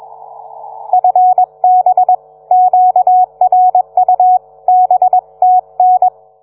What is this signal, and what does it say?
Morse code, heard as a keyed tone, about 21 wpm, 730 Hz. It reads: FB QRU BTN